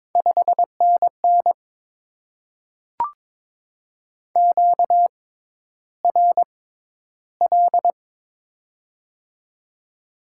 Morse code, heard as a keyed tone, about 22 wpm, spaced as for 11 wpm, 700 Hz. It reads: 5NN QRL
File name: Top 20 QSO Elements - 6x Character Spacing - 22wpm